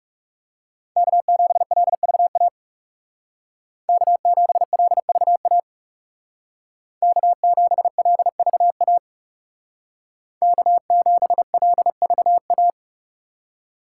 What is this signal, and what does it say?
Morse code, heard as a keyed tone, about 39 wpm, 700 Hz. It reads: K7LVA K7LVA K7LVA K7LVA